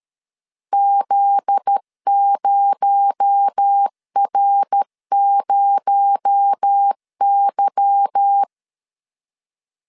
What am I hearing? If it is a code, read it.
Morse code, heard as a keyed tone, about 13 wpm, 800 Hz. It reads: Z0R0Y
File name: CW / DET / 1 9 1500C S A 12wpm 3800